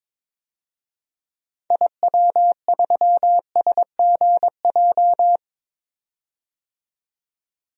Morse code, heard as a keyed tone, about 22 wpm, 700 Hz. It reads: IW3SGJ
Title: DXCC-100 Call Signs - 6 Characters - 22wpm